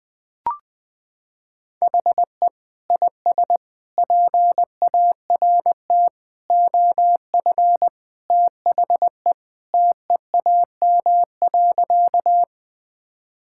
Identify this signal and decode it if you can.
Morse code, heard as a keyed tone, about 20 wpm, 700 Hz. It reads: HE IS PART OF THE TEAM.
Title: Sentences with 3-Word Phrases - 20wpm